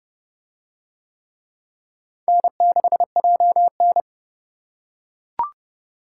Morse code, heard as a keyed tone, about 28 wpm, 700 Hz. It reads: N6JN